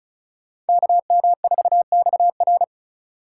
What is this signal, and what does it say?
Morse code, heard as a keyed tone, about 37 wpm, 700 Hz. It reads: KM4XR
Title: Speed Racing - USA 2x2 Call Signs 30wpm